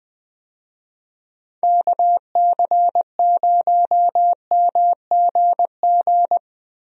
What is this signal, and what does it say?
Morse code, heard as a keyed tone, about 20 wpm, 700 Hz. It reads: KC0MGG
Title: USA Call Signs - 6 Characters - Repeated 2x - 20wpm